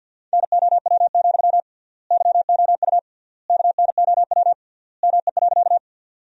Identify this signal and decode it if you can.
Morse code, heard as a keyed tone, about 47 wpm, 700 Hz. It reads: NOW, YOU KNOW ME.
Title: 3x Word-Spacing - Sentences from Top 200 Words - 50wpm